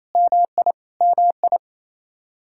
Morse code, heard as a keyed tone, about 28 wpm, 700 Hz. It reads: MI MI
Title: Practice QSO - Exchange 1 - Beginning - 28wpm